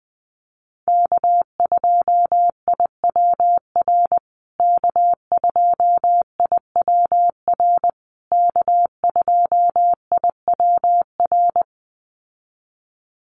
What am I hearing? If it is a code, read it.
Morse code, heard as a keyed tone, about 20 wpm, 700 Hz. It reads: K2IWR K2IWR K2IWR
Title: USA Call Signs Repeated 3x 20wpm